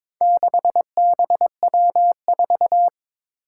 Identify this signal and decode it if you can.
Morse code, heard as a keyed tone, about 23 wpm, 700 Hz. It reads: 6BW4
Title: Four Challenging Letter-Number Combinations 22wpm